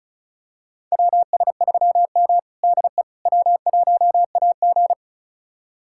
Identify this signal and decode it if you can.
Morse code, heard as a keyed tone, about 36 wpm, 700 Hz. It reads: WS3M DE W1AG
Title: USA Callsign DE Callsign - 4 Characters - 35wpm